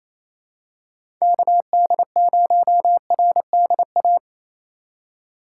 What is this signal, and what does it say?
Morse code, heard as a keyed tone, about 28 wpm, 700 Hz. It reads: KD0RDA
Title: USA Call Signs - 6 Characters - 3x Character Spacing - 28wpm